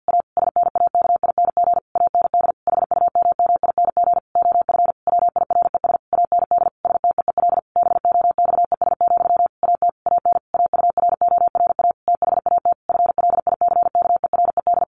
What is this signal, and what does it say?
Morse code, heard as a keyed tone, about 50 wpm, 700 Hz. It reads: A SNAKING AND HUMMING OF WIRES AND STEEL BOXES, AT AN AURORA THAT FLICKERED
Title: A Sound Of Thunder 50wpm